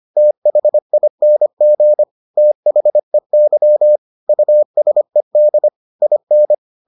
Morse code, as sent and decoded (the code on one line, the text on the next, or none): - .... .. -. --. / - .... -.-- / ..- ... -.. / .. -.
THING THEY USED IN